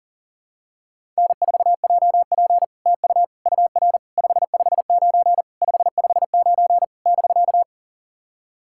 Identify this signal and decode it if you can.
Morse code, heard as a keyed tone, about 40 wpm, 700 Hz. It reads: N4JP TU UR 559 559 <BK>
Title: SOTA QSO Practice 40wpm